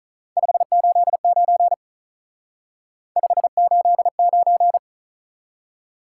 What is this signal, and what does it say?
Morse code, heard as a keyed tone, about 44 wpm, 700 Hz. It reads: 589 589